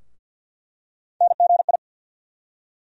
Morse code, heard as a keyed tone, about 47 wpm, 700 Hz. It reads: NGI
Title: USA Call Sign Postfixes - 3 Letters 50wpm